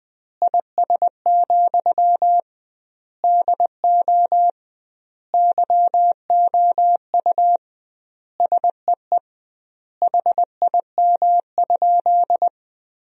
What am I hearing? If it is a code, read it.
Morse code, heard as a keyed tone, about 20 wpm, 700 Hz. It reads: IS, DO YOU SEE HIM?
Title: 2x Word-Spacing - Sentences from Top 100 Words - 20wpm